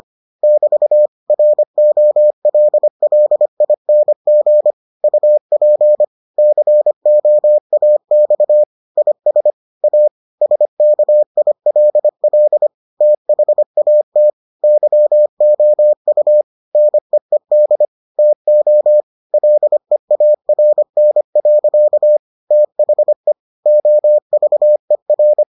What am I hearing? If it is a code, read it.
Morse code, heard as a keyed tone, about 25 wpm, 600 Hz. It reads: <BT> ROLLING UP COAX IS A SKILL THAT YOU NEED TO LEARN. THE OVER